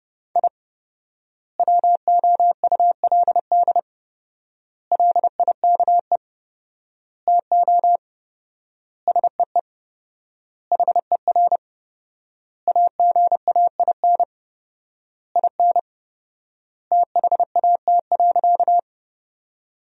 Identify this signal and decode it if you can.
Morse code, heard as a keyed tone, about 30 wpm, 700 Hz. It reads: I WOULD LIKE TO SEE HER AGAIN IN THAT.
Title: 4x Word-Spacing - Sentences from Top 200 Words - 30wpm